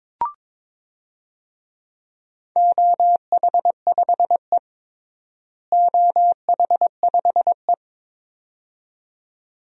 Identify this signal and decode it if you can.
Morse code, heard as a keyed tone, about 22 wpm, 700 Hz. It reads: OH5E OH5E
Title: DXCC-100 Call Signs - 4 Characters - Repeated 2x - 22wpm